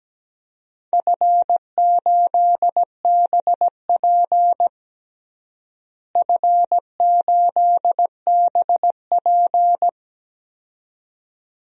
Morse code, heard as a keyed tone, about 17 wpm, 700 Hz. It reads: F8BP F8BP
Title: DXCC-100 Call Signs - 4 Characters - Repeated 2x - 17wpm